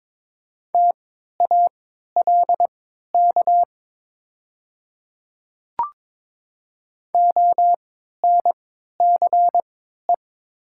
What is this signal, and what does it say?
Morse code, heard as a keyed tone, about 22 wpm, spaced as for 12 wpm, 700 Hz. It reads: TALK ONCE
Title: 3x Character Spacing - Top 500 Words - 22wpm